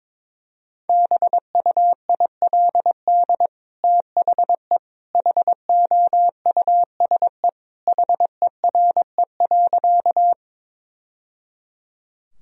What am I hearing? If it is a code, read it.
Morse code, heard as a keyed tone, about 22 wpm, 700 Hz. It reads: BUILD THE HOUSE HERE.
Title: Sentences from Top 1000 Words - Review Part 1 - 22wpm